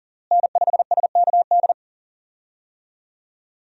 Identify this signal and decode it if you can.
Morse code, heard as a keyed tone, about 40 wpm, 700 Hz. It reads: N5SKD
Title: USA Call Signs - 5 Characters - Repeated 2x - 40wpm